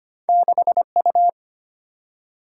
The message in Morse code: 6U

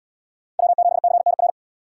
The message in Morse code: W7CNR

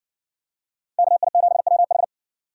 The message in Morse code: KI7WH

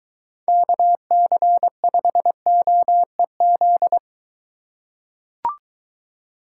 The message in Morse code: KC5OEZ

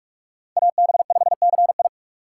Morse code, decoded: AB5CI

